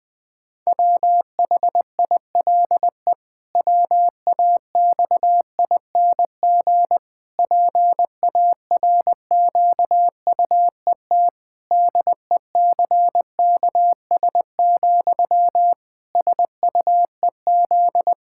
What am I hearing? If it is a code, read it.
Morse code, heard as a keyed tone, about 20 wpm, 700 Hz. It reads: WHILE WAXING PARQUET DECKS, SUEZ
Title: Pangram Challenge v2 - 20wpm